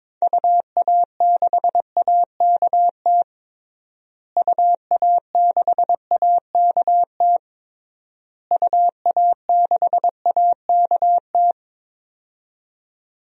Morse code, heard as a keyed tone, about 22 wpm, 700 Hz. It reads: UA6AKT UA6AKT UA6AKT
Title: DXCC-100 Call Signs - 6 Characters - Repeated 3x - 22wpm